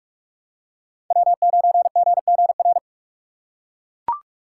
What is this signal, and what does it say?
Morse code, heard as a keyed tone, about 41 wpm, 700 Hz. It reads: W9GGR
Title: USA Call Signs - 2x Character Spacing - 45wpm